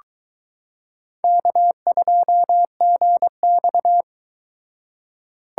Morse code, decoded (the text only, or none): K2GX